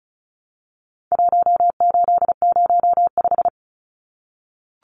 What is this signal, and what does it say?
Morse code, heard as a keyed tone, about 34 wpm, 700 Hz. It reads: 1805
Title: Numbers - Contest Style - 4-Digit 35wpm